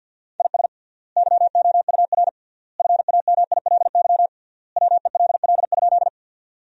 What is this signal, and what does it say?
Morse code, heard as a keyed tone, about 50 wpm, 700 Hz. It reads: IS YOUR FAMILY WELL?
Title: 3x Word-Spacing - Sentences from Top 400 Words - 50wpm